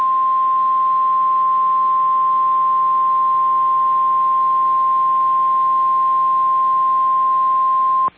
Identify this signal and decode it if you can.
Emergency Alert System (SAME) tones (the 1050 Hz attention tone). It is heard from 1.1 to 7.6 s.